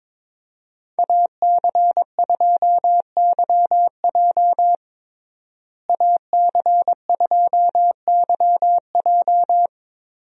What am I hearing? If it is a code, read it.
Morse code, heard as a keyed tone, about 22 wpm, 700 Hz. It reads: AC2YJ AC2YJ
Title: USA Call Signs - 5 Characters - Repeated 2x - 22wpm